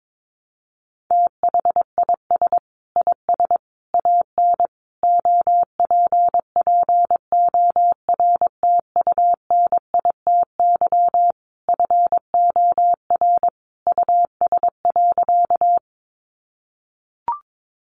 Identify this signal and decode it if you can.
Morse code, heard as a keyed tone, about 22 wpm, 700 Hz. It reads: THIS IS AN OPPORTUNITY FOR US.
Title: Sentences from Top 1000 Words 22wpm